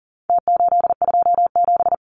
Morse code, heard as a keyed tone, about 40 wpm, 700 Hz. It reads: T827
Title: Numbers - Contest Style - 4-Digit 40wpm